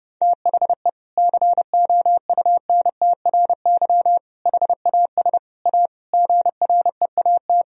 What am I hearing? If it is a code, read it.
Morse code, heard as a keyed tone, about 32 wpm, 700 Hz. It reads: THE COUNTRY HAS A GREAT
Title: Sentences from Top 1000 Words - Review Part 4 - 30wpm